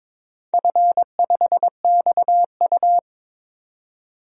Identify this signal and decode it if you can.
Morse code, heard as a keyed tone, about 22 wpm, 700 Hz. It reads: F5XU
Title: DXCC-100 Call Signs - 4 Characters - 22wpm